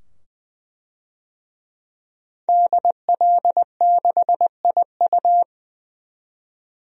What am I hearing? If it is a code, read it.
Morse code, heard as a keyed tone, about 20 wpm, 700 Hz. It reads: DL6IU
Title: DXCC-100 Call Signs - 5 Characters - 20wpm